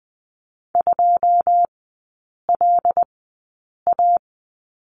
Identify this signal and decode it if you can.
Morse code, heard as a keyed tone, about 20 wpm, spaced as for 8 wpm, 700 Hz. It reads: 2LA
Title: 114 Lesson 19e - Letter w - Three Characters 20wpm